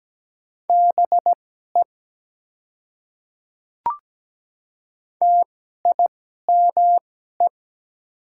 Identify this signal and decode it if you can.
Morse code, heard as a keyed tone, about 17 wpm, spaced as for 12 wpm, 700 Hz. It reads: BE TIME
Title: 2x Character Spacing - Top 100 Words - 17wpm